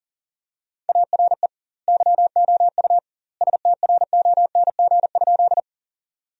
Morse code, heard as a keyed tone, about 40 wpm, 700 Hz. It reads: ARE YOU STRONG?
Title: 2x Word-Spacing - Sentences from Top 500 Words - 40wpm